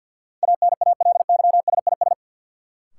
Morse code, heard as a keyed tone, about 52 wpm, 700 Hz. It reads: ANALYSIS
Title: Sentences from Top 1400 Words - New Word Drill Alt - 50wpm